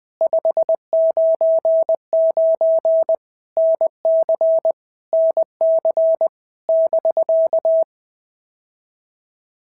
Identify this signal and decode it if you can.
Morse code, heard as a keyed tone, about 20 wpm, 650 Hz. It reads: 599 NC NC <BK>
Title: POTA QSO Practice - 20wpm